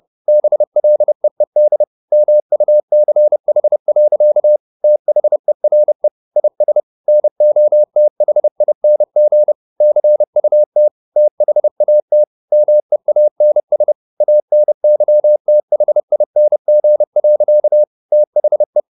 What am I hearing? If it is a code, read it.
Morse code, heard as a keyed tone, about 30 wpm, 600 Hz. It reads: BLEED MUCH. THERE IS NOTHING CUT THAT MEANS ANYTHING. THE